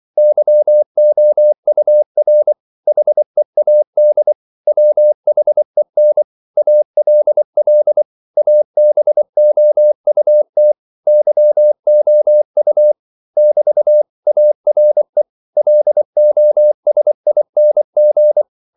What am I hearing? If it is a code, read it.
Morse code, heard as a keyed tone, about 24 wpm, 600 Hz. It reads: YOUR HEAD WHEN ALL ABOUT YOU <BT> ARE LOSING
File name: CW / Practice / If24